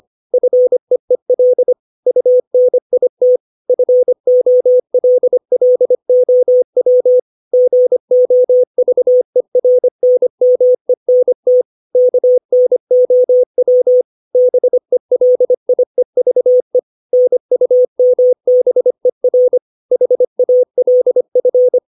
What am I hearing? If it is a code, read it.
Morse code, heard as a keyed tone, about 25 wpm, 500 Hz. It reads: FEEL UNIT FOLLOW GOVERNMENT KNOW BELIEVE NUMBER HALF
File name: 632 words / 632 Words 25WPM